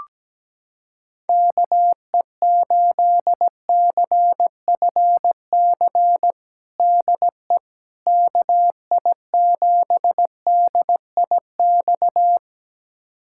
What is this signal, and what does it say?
Morse code, heard as a keyed tone, about 17 wpm, 700 Hz. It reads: KE8CFC DE KI7DIX
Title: USA Callsign DE Callsign - 6 Characters - 17wpm